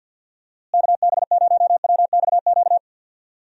Morse code, heard as a keyed tone, about 48 wpm, 700 Hz. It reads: KB0WXQ
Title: USA Call Signs - 6 Characters - Repeated 2x - 50wpm